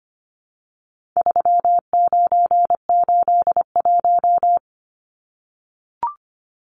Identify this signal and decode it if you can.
Morse code, heard as a keyed tone, about 25 wpm, 700 Hz. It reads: 3981
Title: Numbers - 4-Digit 25wpm